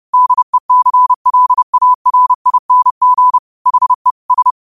Morse code, heard as a keyed tone, about 30 wpm, 1000 Hz. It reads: DECLARING HE S
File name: qotdfiles / 30wpm